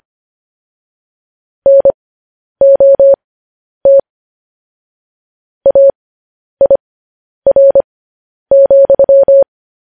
Morse code, heard as a keyed tone, about 25 wpm, spaced as for 10 wpm, 550 Hz. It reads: NOT AIR,